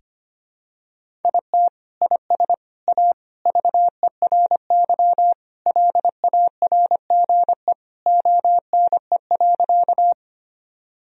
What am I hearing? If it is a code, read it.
Morse code, heard as a keyed tone, about 25 wpm, 700 Hz. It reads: IT IS A VERY LARGE ONE.
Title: Sentences from Top 1000 Words - Review Part 1 - 25wpm